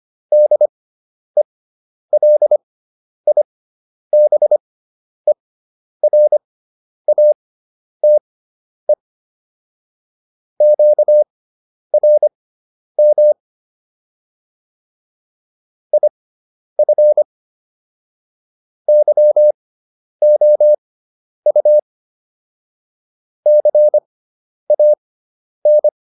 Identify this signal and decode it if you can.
Morse code, heard as a keyed tone, about 25 wpm, spaced as for 10 wpm, 600 Hz. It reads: DELIBERATE QRM IF YOU CAN